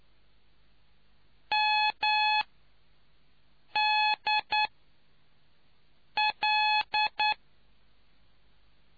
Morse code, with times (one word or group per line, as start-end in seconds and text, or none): 1.5-2.4 M
3.8-4.7 D
6.2-7.3 L